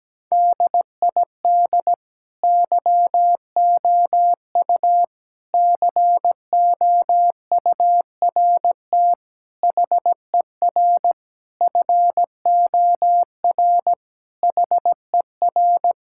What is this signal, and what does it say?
Morse code, heard as a keyed tone, about 17 wpm, 700 Hz. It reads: DID YOU COURT HER FOR HER
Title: Sentences from Top 1600 Words 17wpm